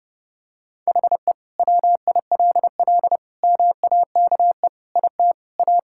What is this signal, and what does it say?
Morse code, heard as a keyed tone, about 30 wpm, 700 Hz. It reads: HE WILL MAKE IT A
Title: Sentences from Top 1000 Words - Review Part 5 - 30wpm